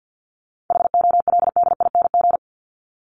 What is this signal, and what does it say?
Morse code, heard as a keyed tone, about 50 wpm, 700 Hz. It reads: HOLDING
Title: ING Suffix - Top 100 Words with Base Words 50wpm